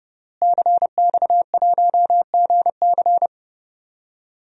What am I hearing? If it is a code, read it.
Morse code, heard as a keyed tone, about 30 wpm, 700 Hz. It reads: CX1GC
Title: DXCC-100 Call Signs - 5 Characters - Repeated 3x - 30wpm